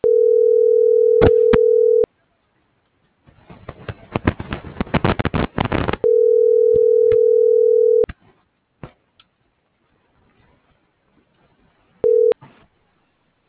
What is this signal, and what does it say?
Ringback tone (North America), 440 + 480 Hz.